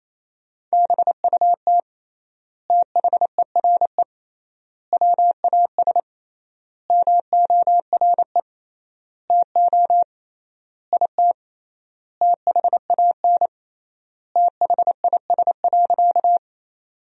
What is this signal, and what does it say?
Morse code, heard as a keyed tone, about 28 wpm, 700 Hz. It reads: BUT THERE WAS MORE TO IT THAN THIS.
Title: 3x Word-Spacing - Sentences from Top 100 Words - 28wpm